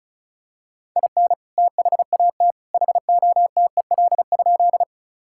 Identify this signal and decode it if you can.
Morse code, heard as a keyed tone, about 34 wpm, 700 Hz. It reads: IN THAT HOTEL?
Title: Sentences from Top 1600 Words 35wpm